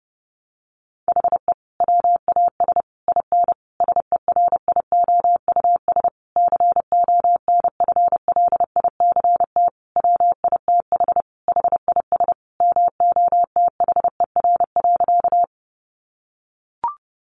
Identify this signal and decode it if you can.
Morse code, heard as a keyed tone, about 30 wpm, 700 Hz. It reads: HE WAS IN SERIOUS CONFLICT WITH HIS MOTHER.
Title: Sentences from Top 1300 Words 30wpm